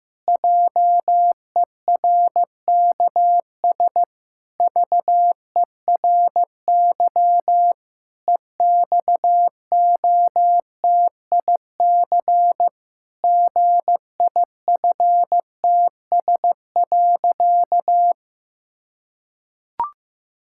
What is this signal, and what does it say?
Morse code, heard as a keyed tone, about 15 wpm, 700 Hz. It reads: JERKS VERY EXOTIC GIFTS.
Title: Pangram Challenge v2 - 15wpm